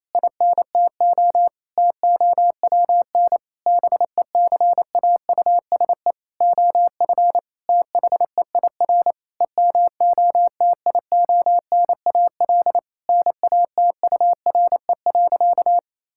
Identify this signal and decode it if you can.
Morse code, heard as a keyed tone, about 28 wpm, 700 Hz. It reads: INTO TOWN BECAUSE OF THEIR EMOTIONAL NATURE.
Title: Sentences from Top 1000 Words - Review Part 5 - 28wpm